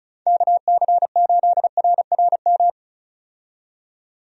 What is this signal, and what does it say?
Morse code, heard as a keyed tone, about 36 wpm, 700 Hz. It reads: KC8RRM
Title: USA Call Signs - 6 Characters - 35wpm